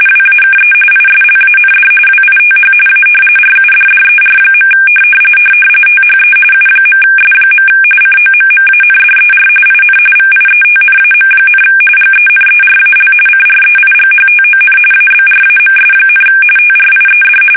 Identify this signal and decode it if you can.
RTTY, about 45.45 baud. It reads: WAKE ISLAND WITH A RUDDER. THE U.S. COMMANDER IN VIETNAM, GENERAL CREIGHTON ABRAMS, TRIED
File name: fsk850 / battleshipnj